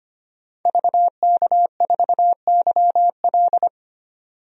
Morse code, heard as a keyed tone, about 25 wpm, 700 Hz. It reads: VK4YL
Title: DXCC-100 Call Signs - 5 Characters - 25wpm